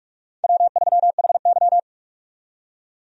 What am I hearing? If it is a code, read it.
Morse code, heard as a keyed tone, about 45 wpm, 700 Hz. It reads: W3HY
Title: USA Call Signs - 4 Characters - 45wpm